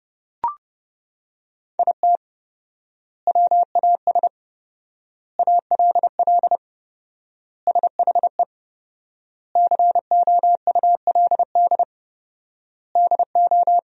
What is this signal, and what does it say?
Morse code, heard as a keyed tone, about 30 wpm, 700 Hz. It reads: IT WAS ALL SHE COULD DO
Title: 4x Word-Spacing - Sentences from Top 400 Words - 30wpm